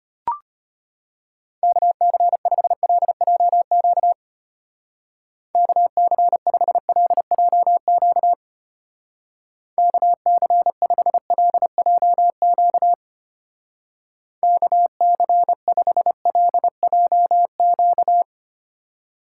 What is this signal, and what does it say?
Morse code, heard as a keyed tone, about 32 wpm, 700 Hz. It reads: KC5LJQ KC5LJQ KC5LJQ KC5LJQ